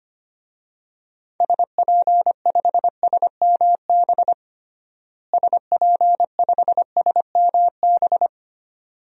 Morse code, heard as a keyed tone, about 25 wpm, 700 Hz. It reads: SP5SMB SP5SMB
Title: DXCC-100 Call Signs - 6 Characters - Repeated 2x - 25wpm